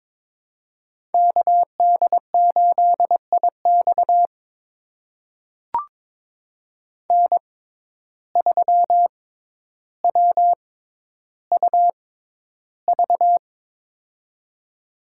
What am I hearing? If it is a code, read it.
Morse code, heard as a keyed tone, about 22 wpm, spaced as for 11 wpm, 700 Hz. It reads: KD8IX N3WUV
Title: USA Call Signs - 5 Characters - 6x Character Spacing - 22wpm